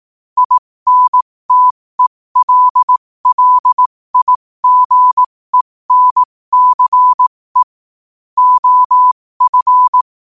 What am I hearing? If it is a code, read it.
Morse code, heard as a keyed tone, about 18 wpm, spaced as for 15 wpm, 1000 Hz. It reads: INTELLIGENCE OF
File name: qotdfiles / 15wpm